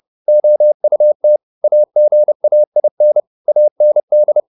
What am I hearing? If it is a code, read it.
Morse code, heard as a keyed tone, about 30 wpm, 600 Hz. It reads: OUT AGAIN AND
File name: the old man and the sea / the old man and the sea 1 30wpm